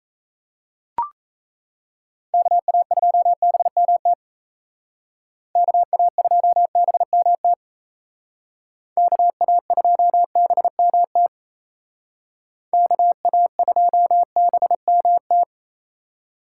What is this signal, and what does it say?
Morse code, heard as a keyed tone, about 35 wpm, 700 Hz. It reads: KA2BMT KA2BMT KA2BMT KA2BMT
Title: Speed Racing - USA 2x3 Call Signs 28wpm